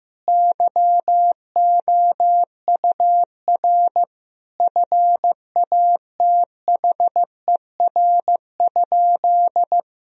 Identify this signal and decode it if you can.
Morse code, heard as a keyed tone, about 15 wpm, 700 Hz. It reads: YOUR FATHER?
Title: Sentences from Top 1000 Words - Review Part 3 - 15wpm